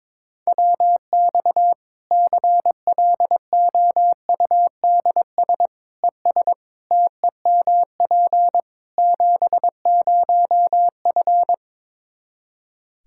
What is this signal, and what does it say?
Morse code, heard as a keyed tone, about 22 wpm, 700 Hz. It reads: WX CLOUDS ES TEMP 70F
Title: Practice QSO - Exchange 2 - Info - 22wpm